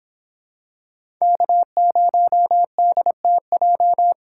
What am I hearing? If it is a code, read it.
Morse code, heard as a keyed tone, about 25 wpm, 700 Hz. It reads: K0DTJ